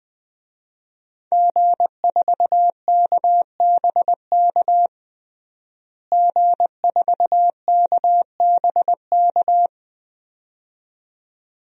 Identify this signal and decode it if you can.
Morse code, heard as a keyed tone, about 20 wpm, 700 Hz. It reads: G4KBK G4KBK
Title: DXCC-100 Call Signs - 5 Characters - Repeated 2x - 20wpm